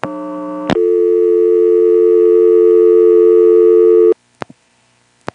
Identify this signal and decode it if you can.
Dial tone (North America), 350 + 440 Hz.